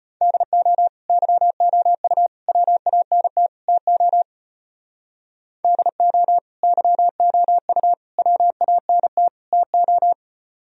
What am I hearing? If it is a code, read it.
Morse code, heard as a keyed tone, about 34 wpm, 700 Hz. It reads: DO YOU WANT TO DO YOU WANT TO